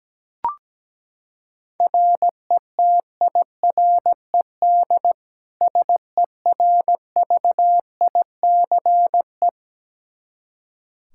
Morse code, heard as a keyed tone, about 17 wpm, 700 Hz. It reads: RETIRED SERVICE
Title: Practice QSO - Exchange 3 - More Info - 17wpm